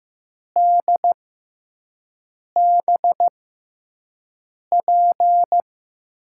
Morse code, heard as a keyed tone, about 15 wpm, spaced as for 5 wpm, 700 Hz. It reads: DBP
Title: USA Call Signs - 6x Character Spacing - 15wpm